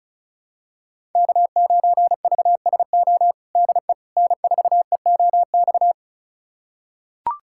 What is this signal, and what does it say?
Morse code, heard as a keyed tone, about 36 wpm, 700 Hz. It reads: K9VSO DE N4EOX
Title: USA Callsign DE Callsign - 5 Characters - 35wpm